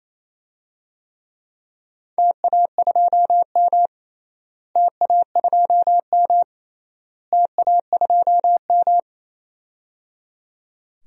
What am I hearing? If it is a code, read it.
Morse code, heard as a keyed tone, about 28 wpm, 700 Hz. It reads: TA2M TA2M TA2M
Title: DXCC-100 Call Signs - 4 Characters - Repeated 3x - 28wpm